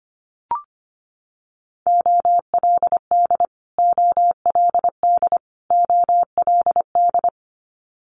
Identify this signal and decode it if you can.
Morse code, heard as a keyed tone, about 25 wpm, 700 Hz. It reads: OLD OLD OLD
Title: Multi-letter Phonemes Repeated 3x 25wpm